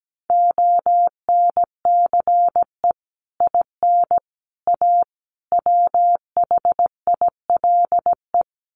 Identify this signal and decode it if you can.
Morse code, heard as a keyed tone, about 17 wpm, 700 Hz. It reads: ONCE IN A WHILE
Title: Sets of 4 Words 17wpm